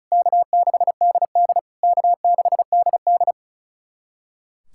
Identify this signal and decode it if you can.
Morse code, heard as a keyed tone, about 36 wpm, 700 Hz. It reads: K6DD K6DD
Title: SOTA QSO Practice 35wpm